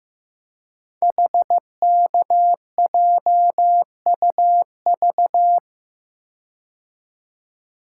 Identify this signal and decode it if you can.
Morse code, heard as a keyed tone, about 15 wpm, 700 Hz. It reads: HKJUV
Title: Code Groups - 5 Letters - 15wpm